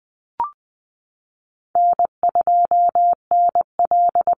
Morse code, heard as a keyed tone, about 20 wpm, 700 Hz. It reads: N2NL
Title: Easy Call Signs Repeated 3x - 20wpm